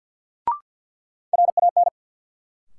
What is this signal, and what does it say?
Morse code, heard as a keyed tone, about 51 wpm, 700 Hz. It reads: RAN
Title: Sentences from Top 500 Words - New Word Drill - 50wpm